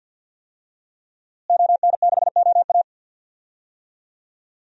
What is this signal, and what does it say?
Morse code, heard as a keyed tone, about 48 wpm, 700 Hz. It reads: ON6OA
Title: DXCC-100 Call Signs - 5 Characters - Repeated 3x - 50wpm